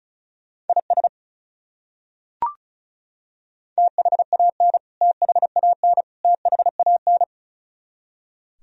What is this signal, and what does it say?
Morse code, heard as a keyed tone, about 34 wpm, 700 Hz. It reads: IS THAN THAN THAN
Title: Top 100 Words Repeated 35wpm